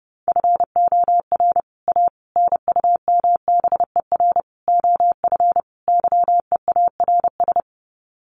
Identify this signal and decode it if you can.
Morse code, heard as a keyed tone, about 30 wpm, 700 Hz. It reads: FOR A NUMBER OF YEARS
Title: Sets of 5 Words Encore 30wpm